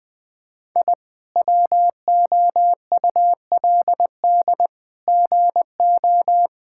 Morse code, heard as a keyed tone, about 20 wpm, 700 Hz. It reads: I WOULD GO
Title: Sentences with 8-Word Phrases - 20wpm